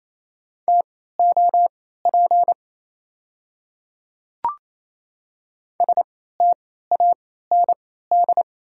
Morse code, heard as a keyed tone, about 28 wpm, spaced as for 16 wpm, 690 Hz. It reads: TOP STAND
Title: 3x Character Spacing - Top 500 Words - 28wpm